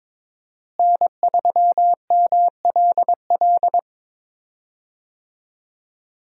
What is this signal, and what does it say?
Morse code, heard as a keyed tone, about 22 wpm, 700 Hz. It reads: N3MLL
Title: USA Call Signs - 5 Characters - Repeated 3x - 22wpm